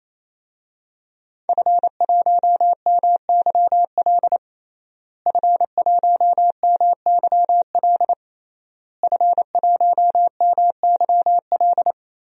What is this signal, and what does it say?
Morse code, heard as a keyed tone, about 28 wpm, 700 Hz. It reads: F1MYL F1MYL F1MYL
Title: DXCC-100 Call Signs - 5 Characters - Repeated 3x - 28wpm